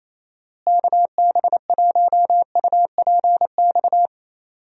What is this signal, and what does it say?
Morse code, heard as a keyed tone, about 28 wpm, 700 Hz. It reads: KB1UPX
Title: USA Call Signs - 6 Characters - Repeated 3x - 28wpm